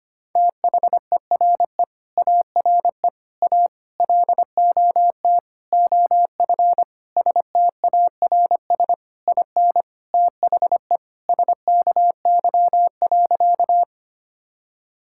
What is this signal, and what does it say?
Morse code, heard as a keyed tone, about 25 wpm, 700 Hz. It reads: THERE ARE A LOT OF STARS IN THE SKY.
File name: practice / 25wpm / Sentences with 4-Word Phrases - 25wpm